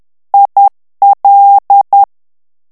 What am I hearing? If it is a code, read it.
Morse code, heard as a keyed tone, about 10 wpm, 800 Hz. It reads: IL
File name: ITALIEN texte clair 2 vitesse 10 mots minute